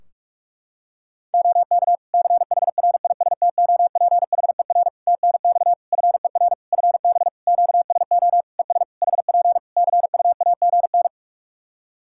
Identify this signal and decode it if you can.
Morse code, heard as a keyed tone, about 46 wpm, 700 Hz. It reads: OK CHRISTOPHER TNX FER FB QSO ES HP CUAGN
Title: Practice QSO - Exchange 4 - Ending - 45wpm